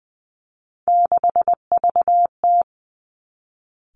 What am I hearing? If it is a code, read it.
Morse code, heard as a keyed tone, about 19 wpm, 700 Hz. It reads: 6VT
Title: Three Challenging Letter-Number Combinations 20wpm